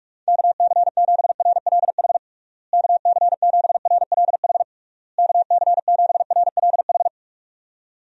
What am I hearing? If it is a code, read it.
Morse code, heard as a keyed tone, about 41 wpm, 700 Hz. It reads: KC7RLH KC7RLH KC7RLH
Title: USA Call Signs - 6 Characters - Repeated 3x - 45wpm